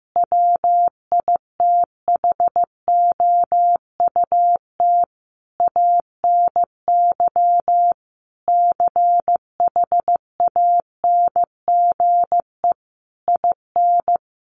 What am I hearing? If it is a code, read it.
Morse code, heard as a keyed tone, about 15 wpm, 700 Hz. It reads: WITHOUT ANY CHANGE IN
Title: Sentences from Top 1200 Words 15wpm